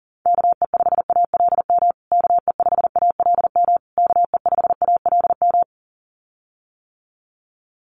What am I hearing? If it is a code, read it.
Morse code, heard as a keyed tone, about 40 wpm, 700 Hz. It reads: KE5ALM KE5ALM KE5ALM
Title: USA Call Signs Repeated 3x 40wpm